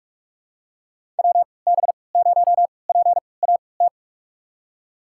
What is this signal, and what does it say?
Morse code, heard as a keyed tone, about 46 wpm, spaced as for 26 wpm, 700 Hz. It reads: WB0PAT
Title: USA Call Signs - 6 Characters - 3x Character Spacing - 45wpm